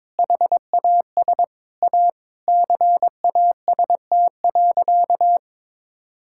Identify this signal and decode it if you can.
Morse code, heard as a keyed tone, about 22 wpm, 700 Hz. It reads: HAS A CAST.
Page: morse-code-ninja-practice-22wpm